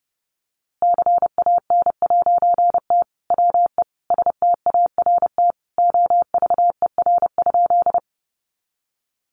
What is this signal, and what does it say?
Morse code, heard as a keyed tone, about 30 wpm, 700 Hz. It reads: CAN'T WE START OVER?
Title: A Sound Of Thunder 30wpm